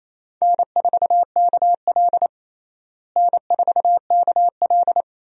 Morse code, heard as a keyed tone, about 28 wpm, 700 Hz. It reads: N4KL N4KL
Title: USA Call Signs - 4 Characters - Repeated 2x - 28wpm